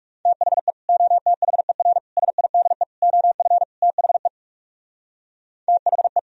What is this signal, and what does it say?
Morse code, heard as a keyed tone, about 37 wpm, 700 Hz. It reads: THE OTHER SIDE OF THE THE
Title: Speed-Racing - Sets of 5 Words 30wpm